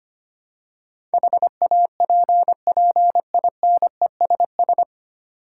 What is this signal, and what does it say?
Morse code, heard as a keyed tone, about 25 wpm, 700 Hz. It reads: HAPPINESS
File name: Nine Letter Words - 25wpm